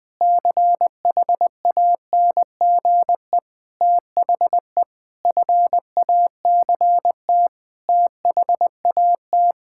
Morse code, heard as a keyed tone, about 21 wpm, 700 Hz. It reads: CHANGE THE FACT THAT
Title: Sentences with 7-Word Phrases - Spoken Beginning - 20wpm